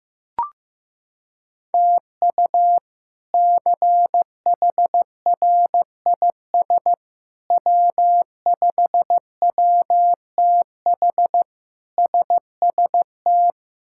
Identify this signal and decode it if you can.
Morse code, heard as a keyed tone, about 16 wpm, 700 Hz. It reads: TU CHRIS W5WTH SST
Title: K1USN SST QSO Practice - 15wpm